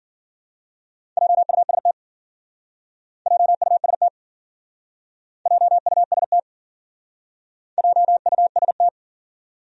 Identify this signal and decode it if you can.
Morse code, heard as a keyed tone, about 50 wpm, 700 Hz. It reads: JUST JUST JUST JUST
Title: Speed-Racing Top 200 Words 40wpm